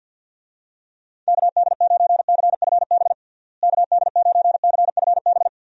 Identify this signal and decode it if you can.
Morse code, heard as a keyed tone, about 50 wpm, 700 Hz. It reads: KD9CFB KD9CFB
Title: USA Call Signs - 6 Characters - Repeated 2x - 50wpm